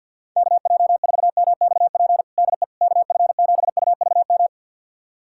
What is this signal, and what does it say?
Morse code, heard as a keyed tone, about 52 wpm, 700 Hz. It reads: KJ4MXP DE KF7UVM
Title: USA Callsign DE Callsign 50wpm